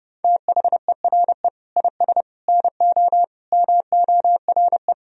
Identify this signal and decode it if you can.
Morse code, heard as a keyed tone, about 30 wpm, 700 Hz. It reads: THERE IS NO MORE
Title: Sentences from Top 1000 Words - Review Part 4 - 30wpm